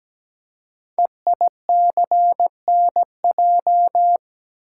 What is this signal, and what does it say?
Morse code, heard as a keyed tone, about 17 wpm, 700 Hz. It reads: EICNJ
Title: Code Groups - 5 Letters - 17wpm